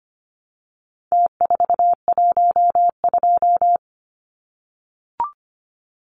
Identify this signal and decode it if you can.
Morse code, heard as a keyed tone, about 25 wpm, 700 Hz. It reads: T412
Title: Numbers - Contest Style - 4-Digit 25wpm